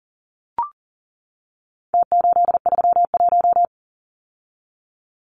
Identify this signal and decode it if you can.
Morse code, heard as a keyed tone, about 40 wpm, 700 Hz. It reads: T831
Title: Numbers - Contest Style - 4-Digit 40wpm